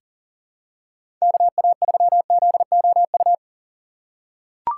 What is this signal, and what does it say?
Morse code, heard as a keyed tone, about 37 wpm, 690 Hz. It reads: KA3ZOU
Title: USA Call Signs - 8x Character Spacing - 40wpm